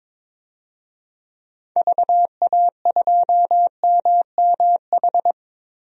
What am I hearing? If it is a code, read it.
Morse code, heard as a keyed tone, about 22 wpm, 700 Hz. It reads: VA2MMH
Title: DXCC-100 Call Signs - 6 Characters - Repeated 2x - 22wpm